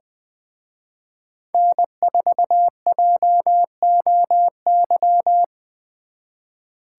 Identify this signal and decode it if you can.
Morse code, heard as a keyed tone, about 24 wpm, 700 Hz. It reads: N4JOY
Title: Speed Racing - USA 1x3 Call Signs 20wpm